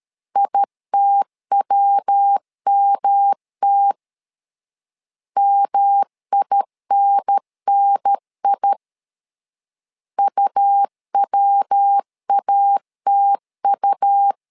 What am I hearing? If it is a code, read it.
Morse code, heard as a keyed tone, about 13 wpm, 800 Hz. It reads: ITWMT MINNI UWATU
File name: LEZIONI / 3 MWRO / 3 MW 12wpm 1633